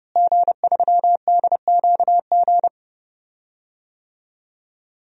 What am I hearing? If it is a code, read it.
Morse code, heard as a keyed tone, about 30 wpm, 700 Hz. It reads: G3DQG
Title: DXCC-100 Call Signs - 5 Characters - Repeated 3x - 30wpm